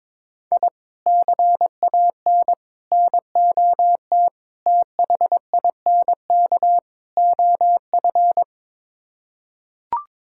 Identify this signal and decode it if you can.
Morse code, heard as a keyed tone, about 22 wpm, 700 Hz. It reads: I CAN NOT THINK OF
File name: practice / 22wpm / Build A Phrase - 5-Word Phrases - 22wpm